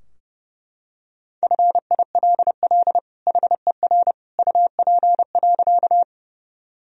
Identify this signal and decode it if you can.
Morse code, heard as a keyed tone, about 30 wpm, 700 Hz. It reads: FILL HER UP.
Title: Sentences from Top 1000 Words - Review Part 3 - 30wpm